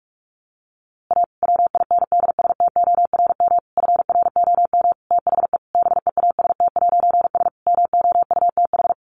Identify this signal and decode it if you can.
Morse code, heard as a keyed tone, about 44 wpm, 700 Hz. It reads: A WINDSTORM FROM THE BEAST'S MOUTH
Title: A Sound Of Thunder 45wpm